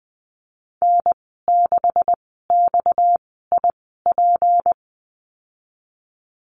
Morse code, heard as a keyed tone, about 20 wpm, spaced as for 14 wpm, 700 Hz. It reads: N6XIP
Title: 231 Lesson 37g - Letter x - Call Signs 20wpm